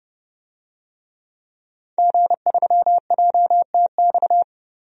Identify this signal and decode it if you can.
Morse code, heard as a keyed tone, about 32 wpm, 700 Hz. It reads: G3JTX
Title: DXCC-100 Call Signs - 5 Characters - Repeated 3x - 30wpm